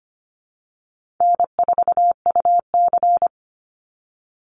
Morse code, heard as a keyed tone, about 25 wpm, 700 Hz. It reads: N4UC
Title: Call Signs - 25wpm